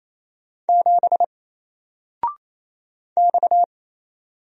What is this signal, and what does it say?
Morse code, heard as a keyed tone, about 27 wpm, 700 Hz. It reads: 7 X
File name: Single-Letter-Number-28wpm